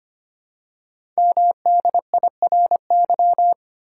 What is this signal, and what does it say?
Morse code, heard as a keyed tone, about 25 wpm, 700 Hz. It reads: MDIRY
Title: Code Groups - 5 Letters - 25wpm